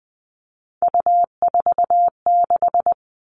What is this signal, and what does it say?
Morse code, heard as a keyed tone, about 20 wpm, 700 Hz. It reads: U46
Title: All Cut Numbers - 3-Digit 20wpm